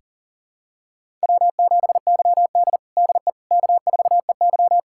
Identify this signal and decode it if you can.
Morse code, heard as a keyed tone, about 37 wpm, 700 Hz. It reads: W7YD DE K4EY
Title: USA Callsign DE Callsign - 4 Characters - 40wpm